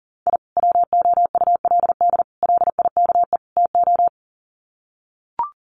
Morse code, heard as a keyed tone, about 38 wpm, 700 Hz. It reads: I WOULD LIKE TO
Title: Sets of 4 Words 40wpm